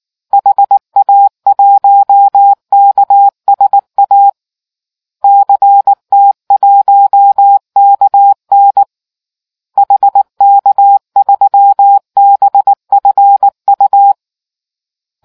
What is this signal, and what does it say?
Morse code, heard as a keyed tone, about 19 wpm, 800 Hz. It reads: HA1KSA CT1KN HK3BFU